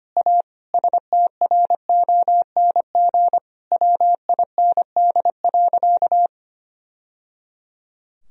Morse code, heard as a keyed tone, about 25 wpm, 700 Hz. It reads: A STRONG WIND.
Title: Sentences from Top 1000 Words - Review Part 4 - 25wpm